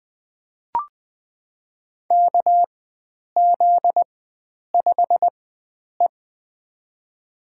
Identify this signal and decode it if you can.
Morse code, heard as a keyed tone, about 20 wpm, spaced as for 9 wpm, 700 Hz. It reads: KZ5E
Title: USA Call Signs - 4 Characters - 4x Character Spacing - 20wpm